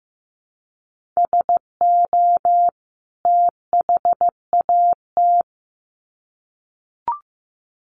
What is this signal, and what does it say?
Morse code, heard as a keyed tone, about 15 wpm, 700 Hz. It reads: SO THAT
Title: Sets of 2 Words 15wpm